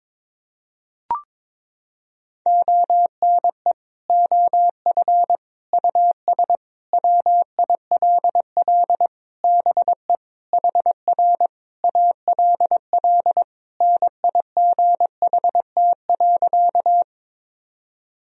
Morse code, heard as a keyed tone, about 21 wpm, 700 Hz. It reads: ONE OF US WILL BE HR ALL NIGHT.
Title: Sentences from Top 300 Words and Abbreviations 22wpm